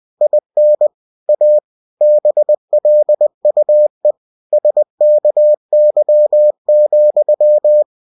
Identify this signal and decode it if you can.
Morse code, heard as a keyed tone, about 20 wpm, 600 Hz. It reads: IN A BLUE SKY,
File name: cw-books / winnie-the-pooh / winnie-20wpm-page-0028